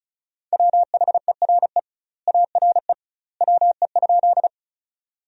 Morse code, heard as a keyed tone, about 36 wpm, 700 Hz. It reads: WHERE ARE WE?
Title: 2x Word-Spacing - Sentences from Top 200 Words - 35wpm